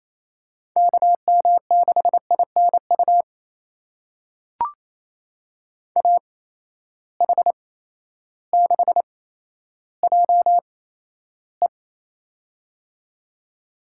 Morse code, heard as a keyed tone, about 28 wpm, spaced as for 13 wpm, 700 Hz. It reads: KM6INU AH6JE
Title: USA Call Signs - 8x Character Spacing - 28wpm